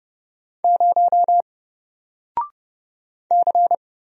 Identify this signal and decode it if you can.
Morse code, heard as a keyed tone, about 30 wpm, 700 Hz. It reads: 0 C
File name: Single-Letter-Number-30wpm